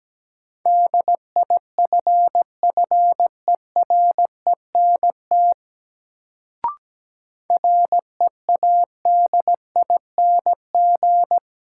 Morse code, heard as a keyed tone, about 17 wpm, 700 Hz. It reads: DIFFERENT READING
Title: Sentences from Top 600 Words - New Word Drill - 17wpm